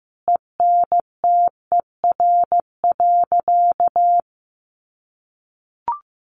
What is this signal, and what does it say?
Morse code, heard as a keyed tone, about 15 wpm, 700 Hz. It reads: ENTER.
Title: Sentences from Top 1300 Words 15wpm